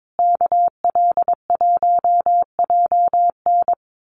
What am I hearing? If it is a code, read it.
Morse code, heard as a keyed tone, about 22 wpm, 700 Hz. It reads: KL1JN
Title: USA Call Signs Repeated 3x 22wpm